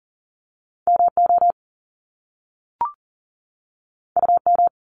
Morse code, heard as a keyed tone, about 40 wpm, 700 Hz. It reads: MO UM
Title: Top Two Letter Combinations in Words 40wpm